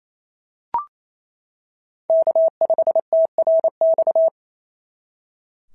Morse code, heard as a keyed tone, about 25 wpm, 650 Hz. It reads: K5TRX
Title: SOTA QSO Practice 28wpm